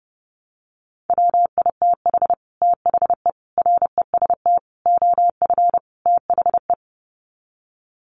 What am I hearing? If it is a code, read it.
Morse code, heard as a keyed tone, about 28 wpm, 700 Hz. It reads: WITH THE REST OF THE
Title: Sets of 5 Words 30wpm